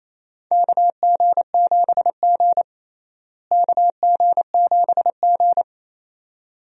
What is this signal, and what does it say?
Morse code, heard as a keyed tone, about 28 wpm, 700 Hz. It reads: KG7G KG7G
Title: USA Call Signs - 4 Characters - Repeated 2x - 28wpm